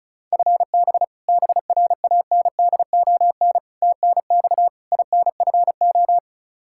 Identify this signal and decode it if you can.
Morse code, heard as a keyed tone, about 34 wpm, 700 Hz. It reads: FB BRANDON TNX INFO
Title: Practice QSO - Exchange 3 - More Info - 35wpm